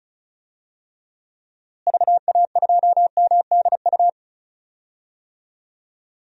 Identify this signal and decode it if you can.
Morse code, heard as a keyed tone, about 34 wpm, 700 Hz. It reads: VA2MDU
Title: DXCC-100 Call Signs - 6 Characters - Repeated 3x - 35wpm